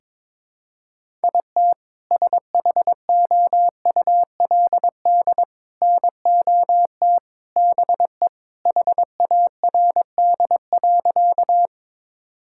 Morse code, heard as a keyed tone, about 22 wpm, 700 Hz. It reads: IT SHOULD NOT BE HARD.
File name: practice / 22wpm / Sentences with 2-Word Phrases - 22wpm